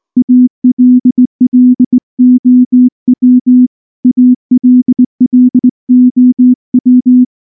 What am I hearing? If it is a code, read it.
Morse code, heard as a keyed tone, about 19 wpm, 250 Hz. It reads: ALLOW ALLOW
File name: morse / morse-training-20WPM-2